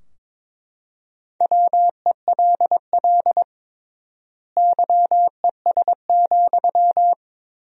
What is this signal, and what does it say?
Morse code, heard as a keyed tone, about 22 wpm, 700 Hz. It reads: WELL YES,
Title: 3x Word-Spacing - Sentences from Top 500 Words - 22wpm